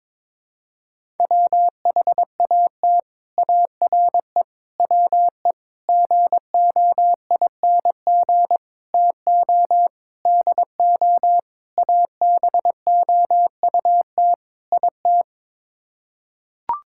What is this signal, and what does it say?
Morse code, heard as a keyed tone, about 22 wpm, 700 Hz. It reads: WHAT ARE WE GOING TO DO ABOUT IT
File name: practice / 22wpm / Sets of 8 Words - 22wpm